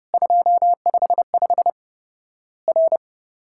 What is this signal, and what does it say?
Morse code, heard as a keyed tone, about 32 wpm, 700 Hz. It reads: 255 R
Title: Worked All Europe DX Contest - QTC Exchange - 30wpm